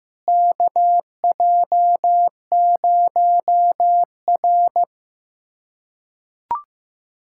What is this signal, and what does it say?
Morse code, heard as a keyed tone, about 16 wpm, 700 Hz. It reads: KJ0R
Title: USA Callsign DE Callsign - 4 Characters - 15wpm